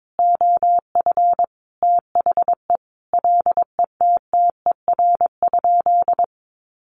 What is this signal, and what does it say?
Morse code, heard as a keyed tone, about 22 wpm, 700 Hz. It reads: OF THE LETTER?
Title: Sentences from Top 900 Words 22wpm